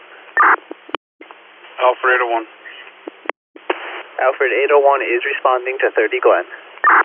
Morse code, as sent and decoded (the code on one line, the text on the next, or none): none